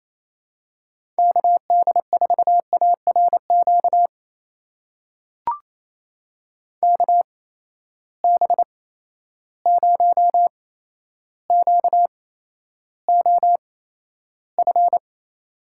Morse code, heard as a keyed tone, about 28 wpm, spaced as for 12 wpm, 700 Hz. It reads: KD4ARQ KB0QOF
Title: USA Call Signs - 6 Characters - 8x Character Spacing - 28wpm